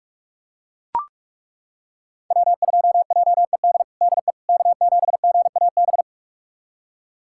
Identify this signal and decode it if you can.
Morse code, heard as a keyed tone, about 42 wpm, 700 Hz. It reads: W2JED DE K7GAB
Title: USA Callsign DE Callsign - 5 Characters - 45wpm